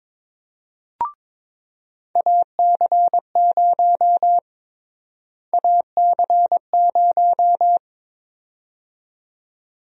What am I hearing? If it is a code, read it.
Morse code, heard as a keyed tone, about 23 wpm, 700 Hz. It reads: AC0 AC0
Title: USA Call Sign Prefixes - Repeated 2x - 22wpm